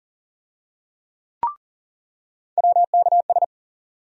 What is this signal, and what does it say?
Morse code, heard as a keyed tone, about 40 wpm, 700 Hz. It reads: WKS
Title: USA Call Sign Postfixes - 3 Letters 40wpm